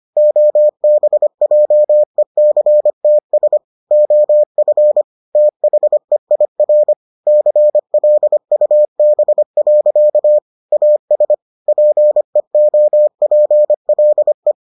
Morse code, heard as a keyed tone, about 25 wpm, 600 Hz. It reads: OBJECTS OF THEIR CLUB. AS PEOPLE